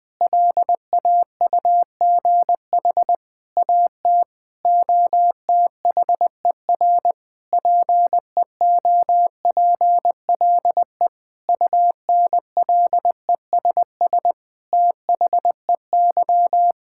Morse code, heard as a keyed tone, about 20 wpm, 700 Hz. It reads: LAUGH AT OTHER PEOPLE UNLESS THEY
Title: Sentences from Top 1000 Words - Review Part 6 - 20wpm